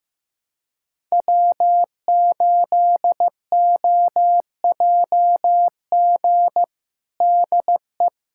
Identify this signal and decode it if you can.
Morse code, heard as a keyed tone, about 15 wpm, 700 Hz. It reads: W8OJG DE
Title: USA Callsign DE Callsign - 5 Characters - 15wpm